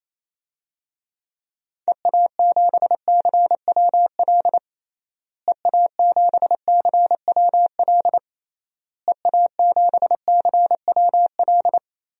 Morse code, heard as a keyed tone, about 28 wpm, 700 Hz. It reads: EA7CWL EA7CWL EA7CWL
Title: DXCC-100 Call Signs - 6 Characters - Repeated 3x - 28wpm